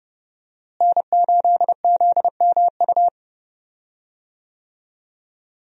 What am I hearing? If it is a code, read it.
Morse code, heard as a keyed tone, about 32 wpm, 700 Hz. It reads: N8ZMU
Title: USA Call Signs - 5 Characters - Repeated 3x - 30wpm